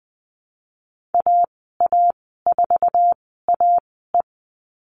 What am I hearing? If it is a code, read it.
Morse code, heard as a keyed tone, about 20 wpm, spaced as for 14 wpm, 700 Hz. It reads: AA4AE
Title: 048 Lesson 9g - Number 4 - Call Signs 20wpm